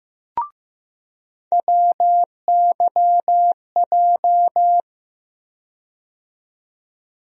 Morse code, heard as a keyed tone, about 15 wpm, 700 Hz. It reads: WYJ
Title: USA Call Sign Postfixes - 3 Letters 15wpm